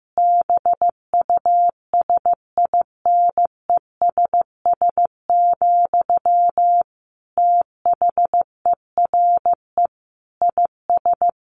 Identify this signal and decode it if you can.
Morse code, heard as a keyed tone, about 15 wpm, 700 Hz. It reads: BUSINESS, THERE IS
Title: Sentences from Top 1400 Words 15wpm